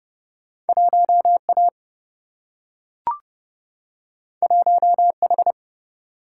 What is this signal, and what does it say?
Morse code, heard as a keyed tone, about 29 wpm, 700 Hz. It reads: 1 A 1 H
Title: Single-Letter Single-Number 30wpm